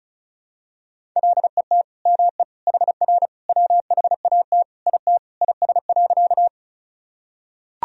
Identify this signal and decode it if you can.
Morse code, heard as a keyed tone, about 34 wpm, 700 Hz. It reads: LET ME HR WHAT IT IS.
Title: Sentences from Top 300 Words and Abbreviations 35wpm